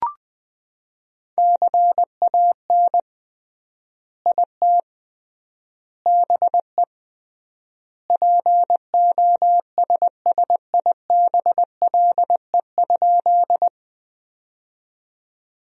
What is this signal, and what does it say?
Morse code, heard as a keyed tone, about 20 wpm, 700 Hz. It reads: CAN IT BE POSSIBLE?
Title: 3x Word-Spacing - Sentences from Top 500 Words - 20wpm